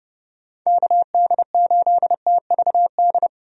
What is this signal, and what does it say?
Morse code, heard as a keyed tone, about 32 wpm, 700 Hz. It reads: KD8TVD
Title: USA Call Signs - 6 Characters - 30wpm